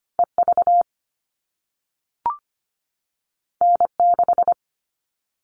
Morse code, heard as a keyed tone, about 25 wpm, 700 Hz. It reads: EV N6
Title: All Cut Numbers - 2-Digit 25wpm